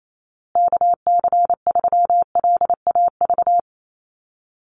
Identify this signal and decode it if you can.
Morse code, heard as a keyed tone, about 28 wpm, 700 Hz. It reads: KC3LAV
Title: USA Call Signs Repeated 3x 28wpm